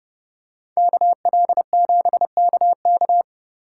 Morse code, heard as a keyed tone, about 29 wpm, 700 Hz. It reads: KL7KK